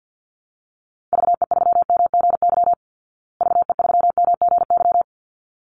Morse code, heard as a keyed tone, about 50 wpm, 700 Hz. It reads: VE3MGY VE3MGY
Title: Call Signs Repeated 3x - 50wpm